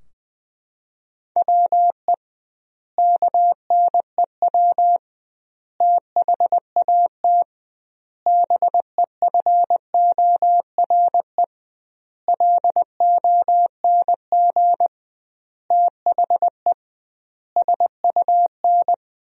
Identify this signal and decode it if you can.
Morse code, heard as a keyed tone, about 21 wpm, 700 Hz. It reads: WE KNEW THAT BEFORE LONG THE SUN
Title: 2x Word-Spacing - Sentences from Top 400 Words - 20wpm